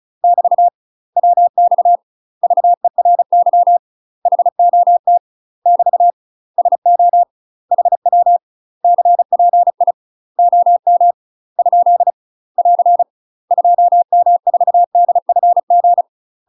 Morse code, heard as a keyed tone, about 34 wpm, 700 Hz. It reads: <BT> WX VERY HOT <BT> SO HW CPI OM ? <AR> 2M4DFG
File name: UK MASTER QSOs TRAINING / 35WPM / UK MASTER QSOs 0042